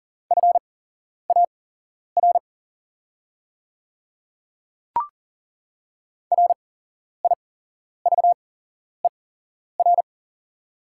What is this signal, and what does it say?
Morse code, heard as a keyed tone, about 39 wpm, spaced as for 11 wpm, 700 Hz. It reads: FAR RIVER
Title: 8x Character Spacing - Top 300 Words - 40wpm